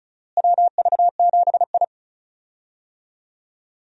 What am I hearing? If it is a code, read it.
Morse code, heard as a keyed tone, about 34 wpm, 700 Hz. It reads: WV7I
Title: USA Call Signs - 4 Characters - Repeated 3x - 35wpm